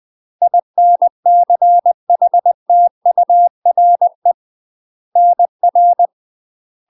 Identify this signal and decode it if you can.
Morse code, heard as a keyed tone, about 20 wpm, 700 Hz. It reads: INCHTURE NR
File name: UK MASTER QSOs TRAINING / 20WPM / UK MASTER QSOs 0007